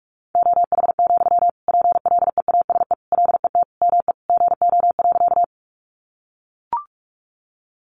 Morse code, heard as a keyed tone, about 44 wpm, 700 Hz. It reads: OH, PLEASE LET ME GO.
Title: Sentences from Top 1500 Words 45wpm